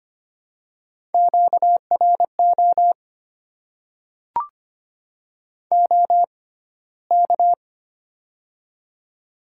Morse code, heard as a keyed tone, about 25 wpm, spaced as for 15 wpm, 700 Hz. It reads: QRO OK
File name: Top 90 QSO Elements - 6x Character Spacing - 25wpm